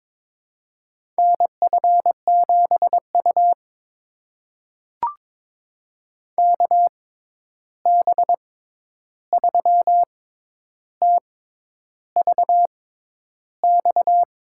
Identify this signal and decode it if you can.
Morse code, heard as a keyed tone, about 22 wpm, spaced as for 10 wpm, 700 Hz. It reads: NF7U KB3TVX
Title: USA Call Signs - 6x Character Spacing - 22wpm